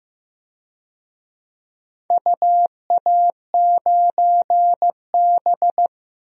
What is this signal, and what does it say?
Morse code, heard as a keyed tone, about 15 wpm, 700 Hz. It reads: UA9B
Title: DXCC-100 Call Signs - 4 Characters - Repeated 3x - 15wpm